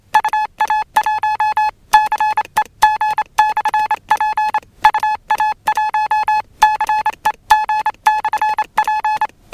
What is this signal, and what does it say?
Morse code, heard as a keyed tone, about 27 wpm, 850 Hz. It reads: UA1CEG/P UA1CEG/P